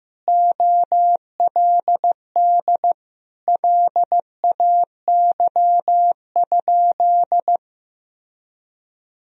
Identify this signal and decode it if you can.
Morse code, heard as a keyed tone, about 15 wpm, 700 Hz. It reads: OLD LAY?
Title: Sentences from Top 400 Words and Abbreviations 15wpm